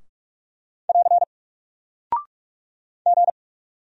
Morse code, heard as a keyed tone, about 44 wpm, 700 Hz. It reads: <AR> G
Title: Single Character (Letter:Number:Punctuation:Prosign) Alt 45wpm